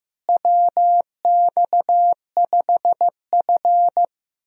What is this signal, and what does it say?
Morse code, heard as a keyed tone, about 16 wpm, 700 Hz. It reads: WX5F
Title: USA Call Signs - 4 Characters - Repeated 3x - 15wpm